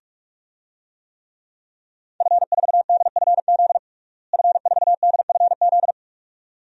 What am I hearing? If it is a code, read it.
Morse code, heard as a keyed tone, about 42 wpm, 700 Hz. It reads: F4DFZ F4DFZ
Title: DXCC-100 Call Signs - 5 Characters - Repeated 2x - 45wpm